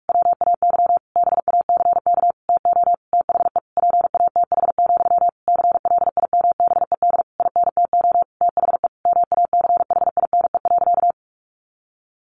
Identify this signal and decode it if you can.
Morse code, heard as a keyed tone, about 46 wpm, 700 Hz. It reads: WAY BACK TO THE PATH, CLIMBED INTO THE MACHINE.
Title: A Sound Of Thunder 45wpm